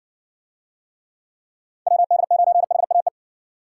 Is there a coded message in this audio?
Yes: Morse code from 1.9 to 3.1 s, heard as a keyed tone, about 60 wpm, 700 Hz. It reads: WD9HAE